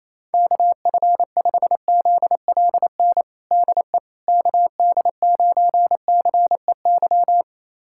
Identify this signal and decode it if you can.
Morse code, heard as a keyed tone, about 27 wpm, 700 Hz. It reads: KF5ZLN DE KD9CEY
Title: USA Callsign DE Callsign 28wpm